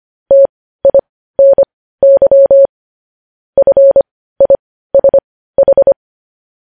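Morse code, heard as a keyed tone, about 25 wpm, spaced as for 15 wpm, 550 Hz. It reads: TINY FISH
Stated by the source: News headlines sent in Morse code at 15 wpm.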